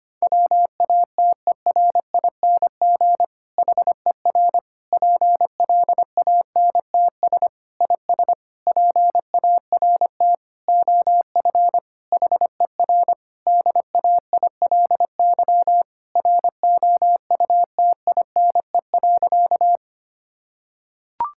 Morse code, heard as a keyed tone, about 25 wpm, 700 Hz. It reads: WATERING HER PLANTS IS PART OF HER DAILY ROUTINE.
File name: practice / 25wpm / Difficult Sentences with Hints - 25wpm